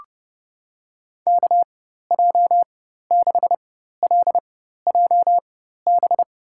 Morse code, heard as a keyed tone, about 30 wpm, spaced as for 14 wpm, 700 Hz. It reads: KJ6LJB
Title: USA Call Signs - 6 Characters - 4x Character Spacing - 30wpm